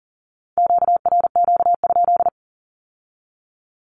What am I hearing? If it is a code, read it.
Morse code, heard as a keyed tone, about 39 wpm, 700 Hz. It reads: QRQ?
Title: Q-Codes with Spoken Meaning - 40wpm